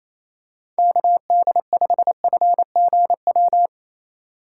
Morse code, heard as a keyed tone, about 28 wpm, 700 Hz. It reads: KD5FGW
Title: USA Call Signs - 6 Characters - 2x Character Spacing - 28wpm